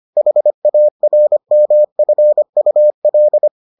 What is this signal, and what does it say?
Morse code, heard as a keyed tone, about 25 wpm, 600 Hz. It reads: HARMFUL